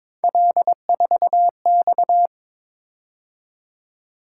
Morse code, heard as a keyed tone, about 22 wpm, 700 Hz. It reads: L4X
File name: Three Letter-Number Combinations - 22wpm